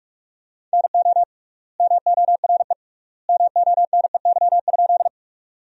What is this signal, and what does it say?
Morse code, heard as a keyed tone, about 46 wpm, 700 Hz. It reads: NO MORE MONEY?
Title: 3x Word-Spacing - Sentences from Top 400 Words - 45wpm